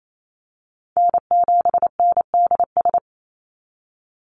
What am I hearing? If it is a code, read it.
Morse code, heard as a keyed tone, about 28 wpm, 700 Hz. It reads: N7NDS
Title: USA Call Signs 28wpm